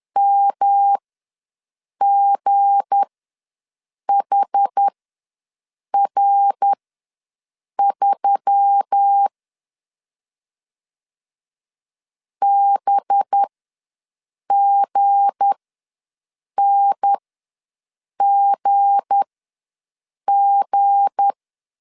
Morse code, heard as a keyed tone, about 10 wpm, spaced as for 6 wpm, 800 Hz. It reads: MGHR3 BGNGG
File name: LEZIONI / 7 GF9Z / 7 G ws 10wpm 3843